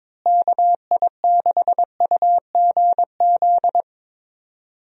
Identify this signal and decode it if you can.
Morse code, heard as a keyed tone, about 22 wpm, 700 Hz. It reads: KI6UGZ
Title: USA Call Signs - 6 Characters - Repeated 2x - 22wpm